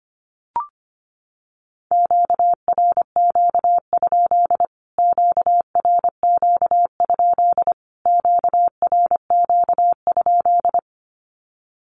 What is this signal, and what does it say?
Morse code, heard as a keyed tone, about 25 wpm, 700 Hz. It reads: QRQ? QRQ? QRQ?
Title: Q-Codes Repeated 3x 25wpm